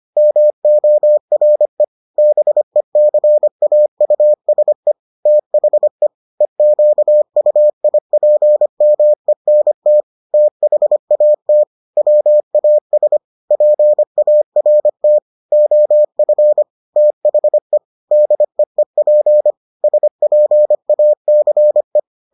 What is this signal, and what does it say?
Morse code, heard as a keyed tone, about 25 wpm, 600 Hz. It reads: MORE BECAUSE THE EQUIPMENT THAT WAS PART OF THE DEEP SPACE